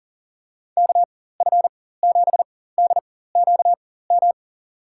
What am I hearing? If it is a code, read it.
Morse code, heard as a keyed tone, about 40 wpm, spaced as for 19 wpm, 700 Hz. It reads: KF7DQM
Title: USA Call Signs - 6 Characters - 4x Character Spacing - 40wpm